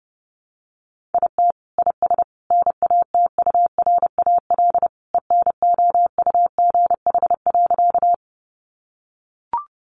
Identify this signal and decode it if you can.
Morse code, heard as a keyed tone, about 30 wpm, 700 Hz. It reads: IT IS NATURAL ENOUGH.
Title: Sentences from Top 700 Words 30wpm